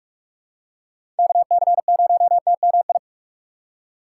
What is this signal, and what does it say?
Morse code, heard as a keyed tone, about 46 wpm, 700 Hz. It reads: KC0TMI
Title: USA Call Signs - 6 Characters - 4x Character Spacing - 45wpm